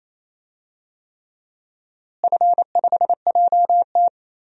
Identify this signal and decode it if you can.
Morse code, heard as a keyed tone, about 28 wpm, 700 Hz. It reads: F5JT
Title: DXCC-100 Call Signs - 4 Characters - Repeated 3x - 28wpm